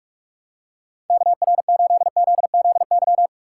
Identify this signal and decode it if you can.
Morse code, heard as a keyed tone, about 42 wpm, 700 Hz. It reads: KR8ZZY
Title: USA Call Signs - 6 Characters - 3x Character Spacing - 45wpm